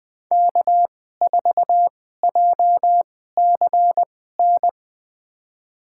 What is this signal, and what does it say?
Morse code, heard as a keyed tone, about 21 wpm, spaced as for 15 wpm, 700 Hz. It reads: K4JCN
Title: USA Call Signs - 5 Characters - 2x Character Spacing - 20wpm